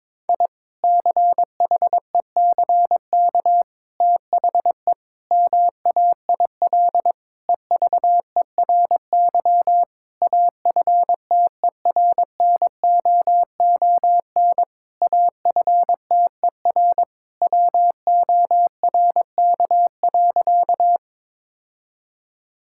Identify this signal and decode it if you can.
Morse code, heard as a keyed tone, about 22 wpm, 700 Hz. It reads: I CHECK THE MAIL EVERY AFTERNOON AFTER WORK.
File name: Difficult Sentences with Hints - 22wpm